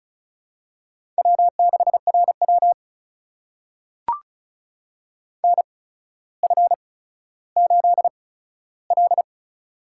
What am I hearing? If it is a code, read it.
Morse code, heard as a keyed tone, about 34 wpm, spaced as for 15 wpm, 700 Hz. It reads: W6RW NF8L
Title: USA Call Signs - 8x Character Spacing - 35wpm